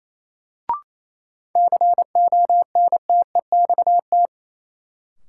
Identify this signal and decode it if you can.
Morse code, heard as a keyed tone, about 28 wpm, 700 Hz. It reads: CONTEXT
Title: Sentences from Top 1000 Words - Review Part 7 - New Word Drill - 28wpm